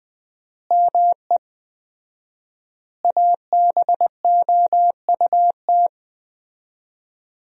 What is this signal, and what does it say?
Morse code, heard as a keyed tone, about 19 wpm, 700 Hz. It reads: ME ABOUT
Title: 4x Word-Spacing - Sentences from Top 200 Words - 20wpm